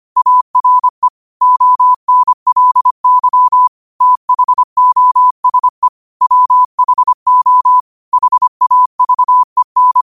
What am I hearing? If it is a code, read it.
Morse code, heard as a keyed tone, about 25 wpm, 1000 Hz. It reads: ARE ONLY THOSE WHO HAVEN
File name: qotdfiles / 25wpm